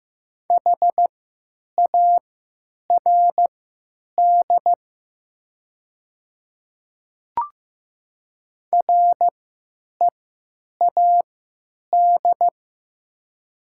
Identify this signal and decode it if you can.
Morse code, heard as a keyed tone, about 15 wpm, spaced as for 9 wpm, 700 Hz. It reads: HARD READ
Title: 3x Character Spacing - Top 500 Words - 15wpm